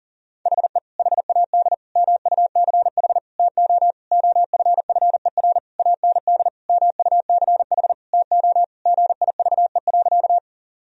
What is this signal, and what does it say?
Morse code, heard as a keyed tone, about 40 wpm, 700 Hz. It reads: HE HAD MUCH TO OFFER AND MUCH TO GIVE.
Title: Sentences from Top 1000 Words - Review Part 4 - 40wpm